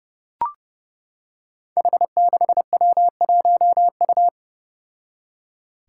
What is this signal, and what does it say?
Morse code, heard as a keyed tone, about 30 wpm, 700 Hz. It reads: H6W1U
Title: Five Challenging Letter-Number Combinations 30wpm